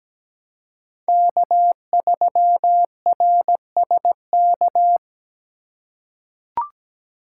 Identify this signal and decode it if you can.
Morse code, heard as a keyed tone, about 17 wpm, 700 Hz. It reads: K3RSK
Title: USA Call Signs - 5 Characters - 3x Character Spacing - 17wpm